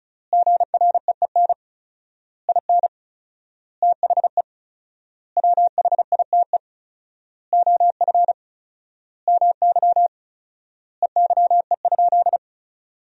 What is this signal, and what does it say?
Morse code, heard as a keyed tone, about 34 wpm, 700 Hz. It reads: GREEN IN THE WHITE OF MY EYE?
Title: 4x Word-Spacing - Sentences from Top 500 Words - 35wpm